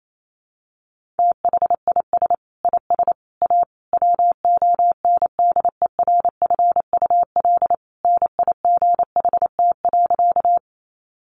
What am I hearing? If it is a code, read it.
Morse code, heard as a keyed tone, about 28 wpm, 700 Hz. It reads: THIS IS A WONDERFUL NIGHT.
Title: Sentences from Top 1100 Words 28wpm